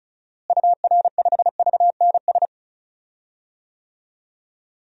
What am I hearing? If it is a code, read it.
Morse code, heard as a keyed tone, about 34 wpm, 700 Hz. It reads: UR5VNS
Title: DXCC-100 Call Signs - 6 Characters - Repeated 2x - 35wpm